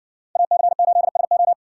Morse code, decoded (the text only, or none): AC8SO